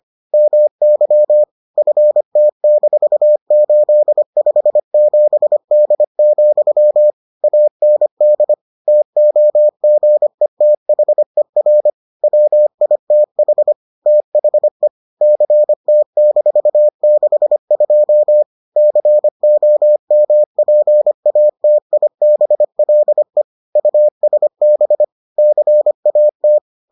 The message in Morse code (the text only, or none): MY FT-857D, AND TOGETHER WITH THE CT-62 COMPATIBLE USB CAT